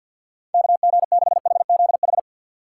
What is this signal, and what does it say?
Morse code, heard as a keyed tone, about 48 wpm, 700 Hz. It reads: KG6HZH
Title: USA Call Signs - 6 Characters - 3x Character Spacing - 50wpm